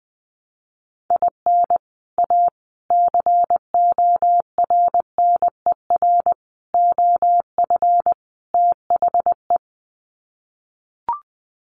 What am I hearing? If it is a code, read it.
Morse code, heard as a keyed tone, about 20 wpm, 700 Hz. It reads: IN A CORNER OF THE
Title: Sets of 5 Words Encore2 20wpm